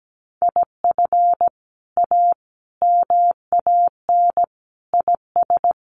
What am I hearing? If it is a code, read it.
Morse code, heard as a keyed tone, about 17 wpm, 700 Hz. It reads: IF A MAN IS
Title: Sentences from Top 1400 Words 17wpm